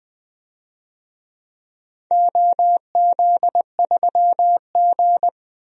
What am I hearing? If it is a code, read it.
Morse code, heard as a keyed tone, about 20 wpm, 700 Hz. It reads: OZ3G
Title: DXCC-100 Call Signs - 4 Characters - 20wpm